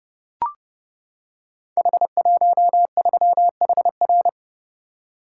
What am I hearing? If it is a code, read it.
Morse code, heard as a keyed tone, about 30 wpm, 700 Hz. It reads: H13HR
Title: Code Groups - 5 Letter:Numbers - 30wpm